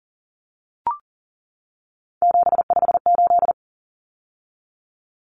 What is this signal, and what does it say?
Morse code, heard as a keyed tone, about 37 wpm, 700 Hz. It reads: 758
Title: Common Cut Numbers - 3-Digit 40wpm